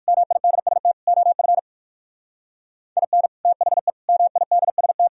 Morse code, heard as a keyed tone, about 58 wpm, 700 Hz. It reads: MIDST OF IN THE MIDST